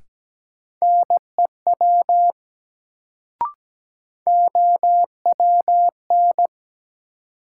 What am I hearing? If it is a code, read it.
Morse code, heard as a keyed tone, about 17 wpm, 700 Hz. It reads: NEW OWN
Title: Sentences from Top 300 Words - New Word Drill - 17wpm